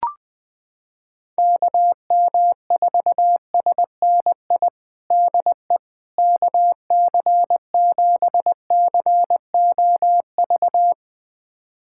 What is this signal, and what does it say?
Morse code, heard as a keyed tone, about 19 wpm, 700 Hz. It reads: KM4SNI DE KC7COV
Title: USA Callsign DE Callsign 20wpm